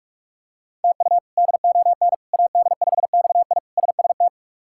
Morse code, heard as a keyed tone, about 46 wpm, 700 Hz. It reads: TU DON AD5XI SST
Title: K1USN SST QSO Practice - 45wpm